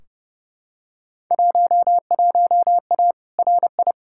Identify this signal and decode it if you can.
Morse code, heard as a keyed tone, about 31 wpm, 700 Hz. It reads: 11A RI
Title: Field Day Exchanges Encore - 30wpm